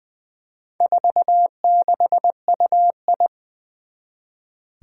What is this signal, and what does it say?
Morse code, heard as a keyed tone, about 20 wpm, 700 Hz. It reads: 46UI
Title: Four Challenging Letter-Number Combinations 20wpm